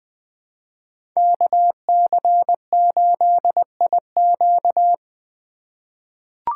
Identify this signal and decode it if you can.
Morse code, heard as a keyed tone, about 20 wpm, 700 Hz. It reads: KC8IQ
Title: 2023 POTA Hunter Call Signs - 20wpm